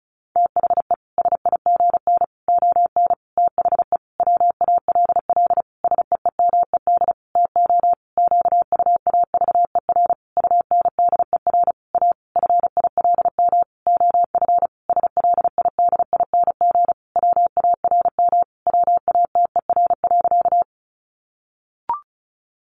Morse code, heard as a keyed tone, about 34 wpm, 700 Hz. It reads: THE SIGN ON THE WALL SEEMED TO QUAVER UNDER A FILM OF SLIDING WARM WATER.
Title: A Sound Of Thunder 35wpm